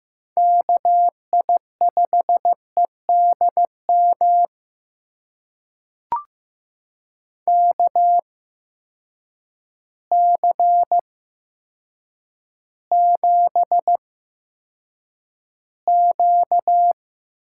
Morse code, heard as a keyed tone, about 15 wpm, spaced as for 8 wpm, 700 Hz. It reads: KI5EDM KC7Q
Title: USA Call Signs - 6 Characters - 8x Character Spacing - 15wpm